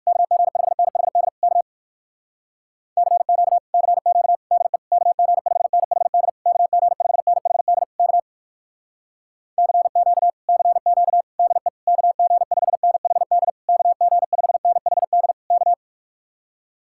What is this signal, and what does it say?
Morse code, heard as a keyed tone, about 66 wpm, spaced as for 58 wpm, 700 Hz. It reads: KG5NHD K CQ CQ DE KG5NHD KG5NHD K CQ CQ DE KG5NHD KG5NHD K